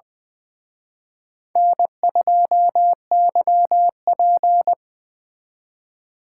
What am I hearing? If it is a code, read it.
Morse code, heard as a keyed tone, about 23 wpm, 700 Hz. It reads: N2YP
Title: Speed Racing - USA 1x2 Call Signs 20wpm